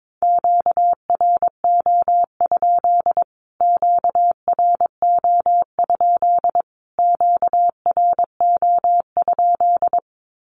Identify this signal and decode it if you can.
Morse code, heard as a keyed tone, about 22 wpm, 700 Hz. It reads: QRO? QRO? QRO?
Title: Q-Codes Repeated 3x 22wpm